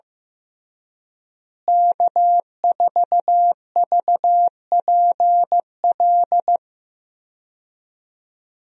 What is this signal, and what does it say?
Morse code, heard as a keyed tone, about 15 wpm, 700 Hz. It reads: K4VPL
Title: USA Call Signs - 5 Characters - Repeated 3x - 15wpm